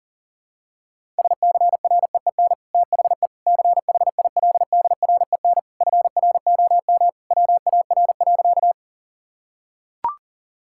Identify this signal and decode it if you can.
Morse code, heard as a keyed tone, about 40 wpm, 700 Hz. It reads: SCREEN THE CHILDREN FROM WAR.
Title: Sentences from Top 1000 Words - Review Part 6 - 40wpm